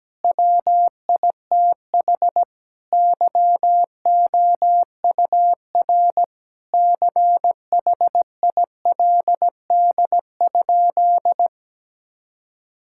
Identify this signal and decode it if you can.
Morse code, heard as a keyed tone, about 17 wpm, 700 Hz. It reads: WITH YOUR CHILD?
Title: Sentences from Top 1000 Words - Review Part 6 - 17wpm